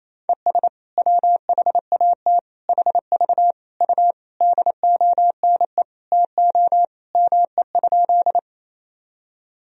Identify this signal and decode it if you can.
Morse code, heard as a keyed tone, about 28 wpm, 700 Hz. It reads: ES WHAT HV U DONE TO ME?
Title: Sentences from Top 500 Words and Abbreviations 28wpm